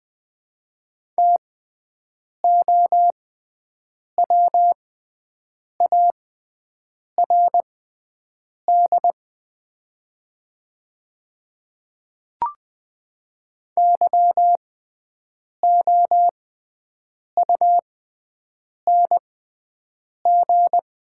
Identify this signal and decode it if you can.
Morse code, heard as a keyed tone, about 20 wpm, spaced as for 7 wpm, 700 Hz. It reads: TOWARD YOUNG
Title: 6x Character Spacing - Top 500 Words - 20wpm